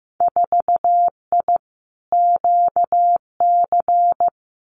Morse code, heard as a keyed tone, about 15 wpm, 700 Hz. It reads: 4I QC
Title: Winter Field Day Exchanges Encore 15wpm